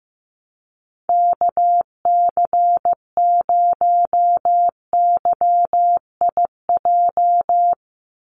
Morse code, heard as a keyed tone, about 15 wpm, 700 Hz. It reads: KC0YIJ
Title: USA Call Signs 15wpm